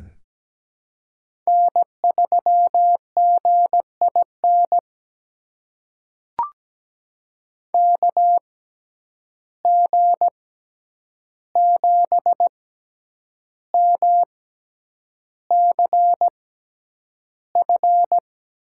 Morse code, heard as a keyed tone, about 17 wpm, spaced as for 8 wpm, 700 Hz. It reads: N3GIN KG7MCF
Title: USA Call Signs - 6x Character Spacing - 17wpm